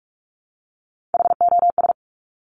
Morse code, heard as a keyed tone, about 46 wpm, 700 Hz. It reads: HOS
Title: Top Three Letter Combinations in Words 45wpm